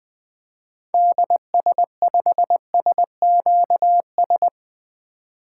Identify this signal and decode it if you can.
Morse code, heard as a keyed tone, about 20 wpm, 700 Hz. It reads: DS5SQS
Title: 2023 SOTA Activator Call Signs - 20wpm